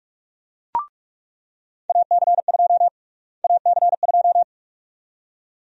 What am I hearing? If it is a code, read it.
Morse code, heard as a keyed tone, about 46 wpm, 700 Hz. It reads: AC2 AC2
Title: USA Call Sign Prefixes - Repeated 2x - 45wpm